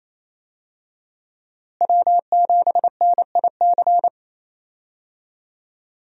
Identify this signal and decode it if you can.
Morse code, heard as a keyed tone, about 27 wpm, 700 Hz. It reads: W7NIC
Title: USA Call Signs - 5 Characters - Repeated 3x - 28wpm